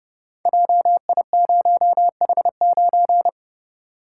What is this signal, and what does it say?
Morse code, heard as a keyed tone, about 30 wpm, 700 Hz. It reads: JI0H9
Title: Code Groups - 5 Letter:Numbers - 30wpm